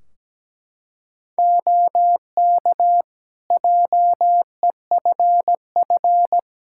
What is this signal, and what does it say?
Morse code, heard as a keyed tone, about 17 wpm, 700 Hz. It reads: OK JEFF
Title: K1USN SST QSO Practice 17wpm